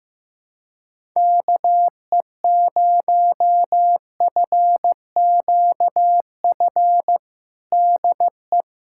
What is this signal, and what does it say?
Morse code, heard as a keyed tone, about 15 wpm, 700 Hz. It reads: KE0FQF DE
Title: USA Callsign DE Callsign - 6 Characters - 15wpm